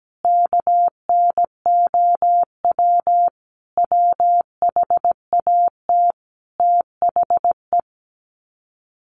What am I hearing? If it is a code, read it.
Morse code, heard as a keyed tone, about 17 wpm, 700 Hz. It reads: KNOW WHAT THE
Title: Sets of 5 Words 17wpm